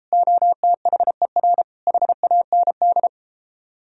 Morse code, heard as a keyed tone, about 34 wpm, 700 Hz. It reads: OTHER HAND